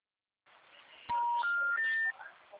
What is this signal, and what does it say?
Special information tone.
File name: out- -2004- - - .455